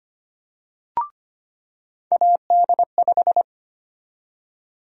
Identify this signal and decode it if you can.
Morse code, heard as a keyed tone, about 25 wpm, 700 Hz. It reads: AD5
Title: USA Call Sign Prefixes - 25wpm